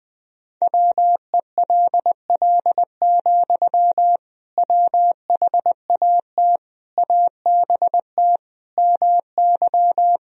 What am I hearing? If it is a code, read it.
Morse code, heard as a keyed tone, about 21 wpm, 700 Hz. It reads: WELL, WHAT ABT MY
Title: Sentences from Top 500 Words and Abbreviations 20wpm